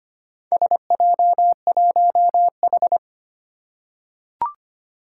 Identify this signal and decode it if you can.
Morse code, heard as a keyed tone, about 25 wpm, 700 Hz. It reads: SJ1H
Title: Four Challenging Letter-Number Combinations 25wpm